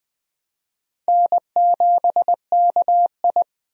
Morse code, heard as a keyed tone, about 20 wpm, 700 Hz. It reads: N7KI
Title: USA Call Signs - 4 Characters - 20wpm